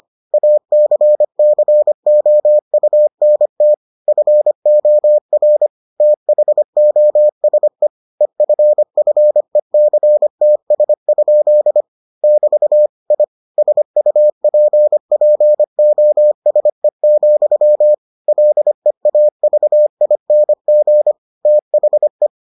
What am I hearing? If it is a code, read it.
Morse code, heard as a keyed tone, about 25 wpm, 600 Hz. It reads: ACCOUNT FOR THOSE EFFECTS? <BT> I SUPPOSE, LEAVING THE